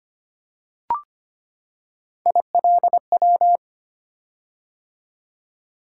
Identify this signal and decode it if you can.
Morse code, heard as a keyed tone, about 25 wpm, 700 Hz. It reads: ILW
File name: Three Letter-Number Combinations - 25wpm